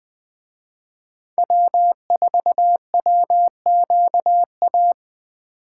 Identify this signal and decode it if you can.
Morse code, heard as a keyed tone, about 20 wpm, 700 Hz. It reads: W4WQA